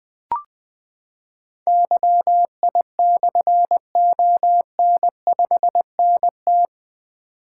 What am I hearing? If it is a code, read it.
Morse code, heard as a keyed tone, about 20 wpm, 700 Hz. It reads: YI/ON5NT
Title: Difficult Call Signs 20wpm